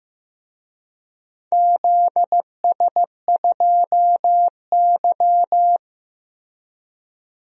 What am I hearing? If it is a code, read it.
Morse code, heard as a keyed tone, about 15 wpm, 700 Hz. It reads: ZS2Y
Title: DXCC-100 Call Signs - 4 Characters - Repeated 3x - 15wpm